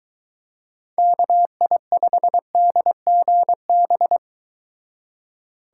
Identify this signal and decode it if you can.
Morse code, heard as a keyed tone, about 23 wpm, 700 Hz. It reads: KI5DGB